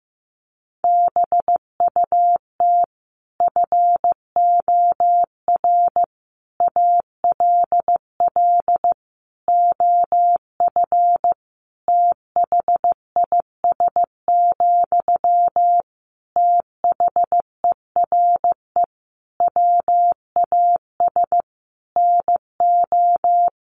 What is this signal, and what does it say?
Morse code, heard as a keyed tone, about 14 wpm, 700 Hz. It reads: BUT FOR ALL OF THIS, THERE WAS NO
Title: Sentences from Top 1300 Words 15wpm